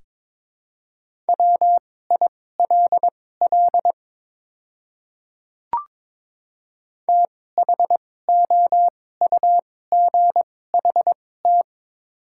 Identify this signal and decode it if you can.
Morse code, heard as a keyed tone, about 22 wpm, spaced as for 16 wpm, 700 Hz. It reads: WILL THOUGHT
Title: 2x Character Spacing - Top 300 Words - 22wpm